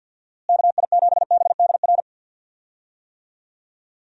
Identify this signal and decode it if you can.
Morse code, heard as a keyed tone, about 46 wpm, 700 Hz. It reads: KI7BDR
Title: USA Call Signs - 6 Characters - Repeated 3x - 50wpm